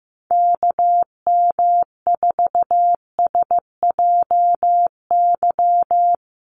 Morse code, heard as a keyed tone, about 15 wpm, 700 Hz. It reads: KM4SJY
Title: USA Call Signs Repeated 2x 15wpm